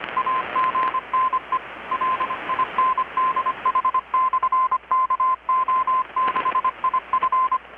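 Morse code, heard as a keyed tone, about 25 wpm, 1040 Hz, with radio noise. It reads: AGNE LINDH/KO6IF
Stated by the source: Morse from KPH's Night of Nights XX maritime radiotelegraphy event, received in Chiapas, Mexico about 3,800 km away on a Kenwood R-600 with a grounded 10 m long-wire antenna.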